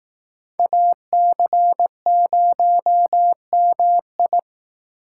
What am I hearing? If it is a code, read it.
Morse code, heard as a keyed tone, about 18 wpm, 700 Hz. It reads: AC0MI
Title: Speed Racing - USA 2x2 Call Signs 15wpm